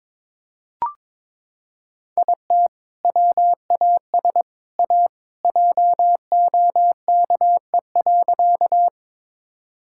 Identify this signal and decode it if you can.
Morse code, heard as a keyed tone, about 21 wpm, 700 Hz. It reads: IT WAS A JOKE.
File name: practice / 22wpm / Sentences with 3-Word Phrases - 22wpm